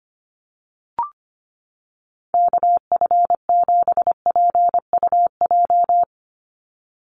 Morse code, heard as a keyed tone, about 25 wpm, 700 Hz. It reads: KF7PUJ
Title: USA Call Signs 25wpm